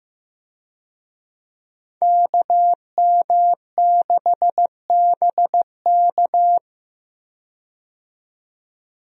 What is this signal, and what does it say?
Morse code, heard as a keyed tone, about 15 wpm, 700 Hz. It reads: KM6BK
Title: USA Call Signs - 5 Characters - 15wpm